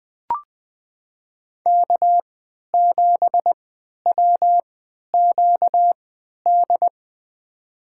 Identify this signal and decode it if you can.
Morse code, heard as a keyed tone, about 20 wpm, spaced as for 11 wpm, 700 Hz. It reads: K7WQD
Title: USA Call Signs - 5 Characters - 3x Character Spacing - 20wpm